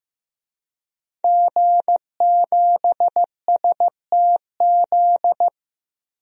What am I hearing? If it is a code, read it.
Morse code, heard as a keyed tone, about 15 wpm, 700 Hz. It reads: G7STZ
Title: DXCC-100 Call Signs - 5 Characters - 15wpm